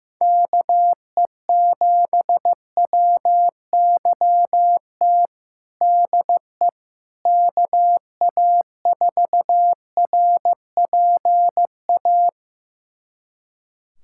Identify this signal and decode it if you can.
Morse code, heard as a keyed tone, about 15 wpm, 700 Hz. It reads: KE7WYT DE KA4RPA
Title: USA Callsign DE Callsign 15wpm